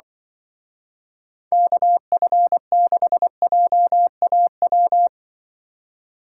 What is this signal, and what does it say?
Morse code, heard as a keyed tone, about 24 wpm, 700 Hz. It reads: KF6JAW
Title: Speed Racing - USA 2x3 Call Signs 20wpm